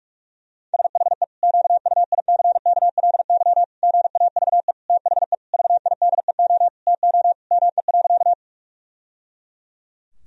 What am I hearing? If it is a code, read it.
Morse code, heard as a keyed tone, about 46 wpm, 700 Hz. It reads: SHE QUICKLY GAVE THE VIDEO TO ME.
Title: Sentences from Top 1000 Words - Review Part 4 - 45wpm